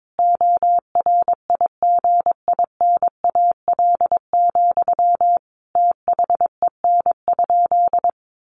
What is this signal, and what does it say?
Morse code, heard as a keyed tone, about 22 wpm, 700 Hz. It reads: ORIGINAL, THEN?
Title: Sentences from Top 900 Words 22wpm